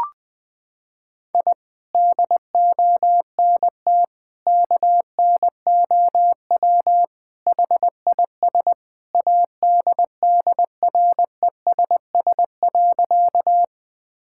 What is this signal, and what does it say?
Morse code, heard as a keyed tone, about 20 wpm, 700 Hz. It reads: I DONT KNOW HIS ADDRESS.
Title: Sentences from Top 1000 Words - Review Part 6 - 20wpm